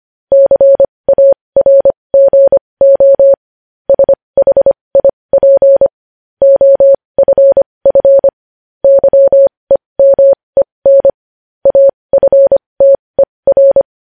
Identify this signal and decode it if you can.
Morse code, heard as a keyed tone, about 25 wpm, spaced as for 20 wpm, 550 Hz. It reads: CARGO SHIP OFF YEMEN AFTER
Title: News Headlines at 20